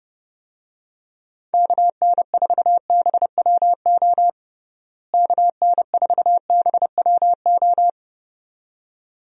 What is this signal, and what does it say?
Morse code, heard as a keyed tone, about 32 wpm, 700 Hz. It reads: KN4BWO KN4BWO
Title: USA Call Signs - 6 Characters - Repeated 2x - 30wpm